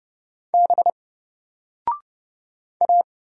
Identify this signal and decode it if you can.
Morse code, heard as a keyed tone, about 29 wpm, 700 Hz. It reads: B A
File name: Single-Letter-Number-30wpm